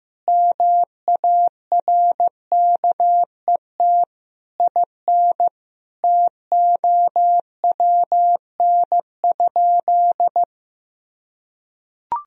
Morse code, heard as a keyed tone, about 15 wpm, 700 Hz. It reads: MARKET IN TOWN?
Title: Sentences from Top 1000 Words - Review Part 3 - 15wpm